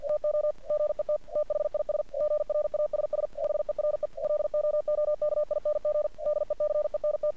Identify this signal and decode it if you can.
Morse code, heard as a keyed tone, about 47 wpm, 600 Hz. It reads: TO GET THIS GRASS HERE LOOKING DECENT
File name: archived / CRSnet